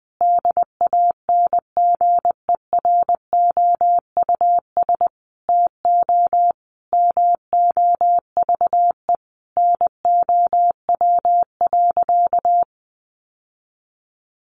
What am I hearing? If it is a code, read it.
Morse code, heard as a keyed tone, about 20 wpm, 700 Hz. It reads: DANGEROUS TO MOVE NOW.
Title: Sentences from Top 1200 Words 20wpm